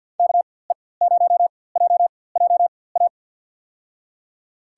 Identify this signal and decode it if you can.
Morse code, heard as a keyed tone, about 48 wpm, spaced as for 23 wpm, 700 Hz. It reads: KE0JJA
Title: USA Call Signs - 6 Characters - 4x Character Spacing - 50wpm